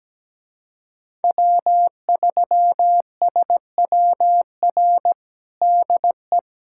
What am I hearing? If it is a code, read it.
Morse code, heard as a keyed tone, about 17 wpm, 700 Hz. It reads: W3SWR DE
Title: USA Callsign DE Callsign - 5 Characters - 17wpm